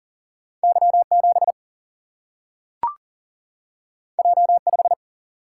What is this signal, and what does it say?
Morse code, heard as a keyed tone, about 39 wpm, 700 Hz. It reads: Y7 J5
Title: Single-Letter Single-Number 40wpm